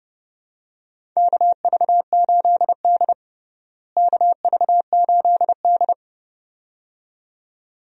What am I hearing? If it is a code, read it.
Morse code, heard as a keyed tone, about 30 wpm, 700 Hz. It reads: KV8D KV8D
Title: USA Call Signs - 4 Characters - Repeated 2x - 30wpm